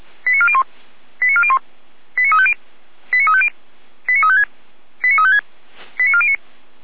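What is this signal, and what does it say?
Five-tone selective call (ZVEI), heard as quick runs of five tones, tones of about 70 ms. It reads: 89401 89401 89350 89350 89356 89356 89409
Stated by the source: ZVEI five-tone selective call 89338 received by SDR.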